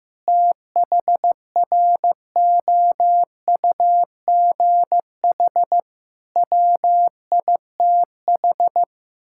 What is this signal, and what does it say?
Morse code, heard as a keyed tone, about 16 wpm, 700 Hz. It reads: THROUGH WITH
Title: Sentences from Top 1000 Words - Review Part 6 - 15wpm